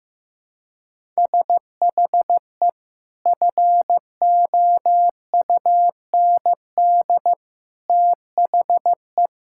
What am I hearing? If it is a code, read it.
Morse code, heard as a keyed tone, about 15 wpm, 700 Hz. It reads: SHE FOUND THE
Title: Sentences from Top 1000 Words - Review Part 6 - 15wpm